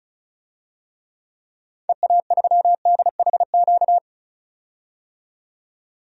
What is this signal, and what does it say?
Morse code, heard as a keyed tone, about 36 wpm, 700 Hz. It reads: EA3DHQ
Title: DXCC-100 Call Signs - 6 Characters - 35wpm